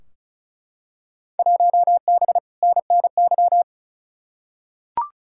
Morse code, heard as a keyed tone, about 36 wpm, 700 Hz. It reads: 1B NNY
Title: Field Day Exchanges Encore - 35wpm